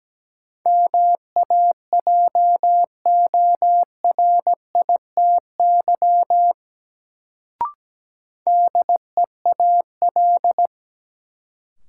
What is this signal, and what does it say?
Morse code, heard as a keyed tone, about 17 wpm, 700 Hz. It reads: MAJORITY DEAL
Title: Sets of 3 Words - New Word Drill - 17wpm